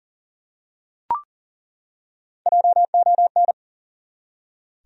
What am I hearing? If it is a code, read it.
Morse code, heard as a keyed tone, about 37 wpm, 700 Hz. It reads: JON
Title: USA Common Names - 40wpm